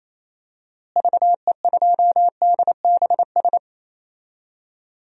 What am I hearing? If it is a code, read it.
Morse code, heard as a keyed tone, about 28 wpm, 700 Hz. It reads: VE2DBS
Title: DXCC-100 Call Signs - 6 Characters - Repeated 2x - 28wpm